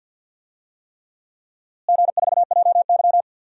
Morse code, heard as a keyed tone, about 47 wpm, 700 Hz. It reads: G4JY
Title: DXCC-100 Call Signs - 4 Characters - Repeated 3x - 50wpm